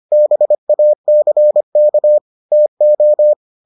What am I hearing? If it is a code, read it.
Morse code, heard as a keyed tone, about 25 wpm, 600 Hz. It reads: BACK TO